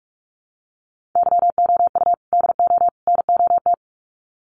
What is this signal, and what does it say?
Morse code, heard as a keyed tone, about 44 wpm, 700 Hz. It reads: YOU DO NOT
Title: Sets of 3 Words 45wpm